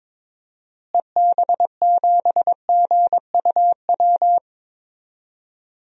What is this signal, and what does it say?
Morse code, heard as a keyed tone, about 21 wpm, 700 Hz. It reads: EB7GUW
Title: DXCC-100 Call Signs - 6 Characters - Repeated 2x - 22wpm